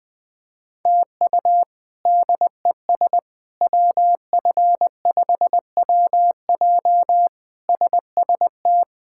Morse code, heard as a keyed tone, about 20 wpm, 700 Hz. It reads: TU DES WF5WJ SST
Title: K1USN SST QSO Practice - 20wpm